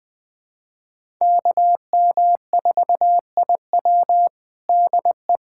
Morse code, heard as a keyed tone, about 21 wpm, 700 Hz. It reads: KM4IW DE
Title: USA Callsign DE Callsign - 5 Characters - 20wpm